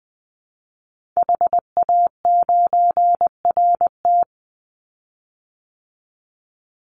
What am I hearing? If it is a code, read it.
Morse code, heard as a keyed tone, about 20 wpm, 700 Hz. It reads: HA9RT
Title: Call Signs Repeated 3x - 20wpm